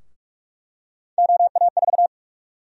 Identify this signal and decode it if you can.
Morse code, heard as a keyed tone, about 46 wpm, 700 Hz. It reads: OA4
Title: USA Call Sign Prefixes - Repeated 2x - 45wpm